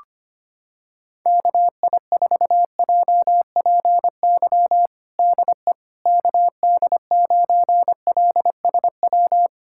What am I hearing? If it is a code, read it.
Morse code, heard as a keyed tone, about 25 wpm, 700 Hz. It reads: KI4JPY DE KD9LSW
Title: USA Callsign DE Callsign 25wpm